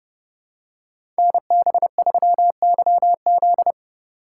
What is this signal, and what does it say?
Morse code, heard as a keyed tone, about 30 wpm, 700 Hz. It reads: NB3YZ
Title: Code Groups - 5 Letter:Numbers - 30wpm